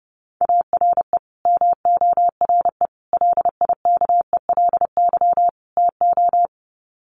Morse code, heard as a keyed tone, about 30 wpm, 700 Hz. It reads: ARE MORE LIKELY TO
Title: Sets of 4 Words 30wpm